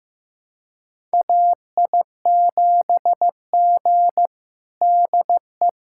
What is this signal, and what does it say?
Morse code, heard as a keyed tone, about 16 wpm, 700 Hz. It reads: AI7G DE
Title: USA Callsign DE Callsign - 4 Characters - 15wpm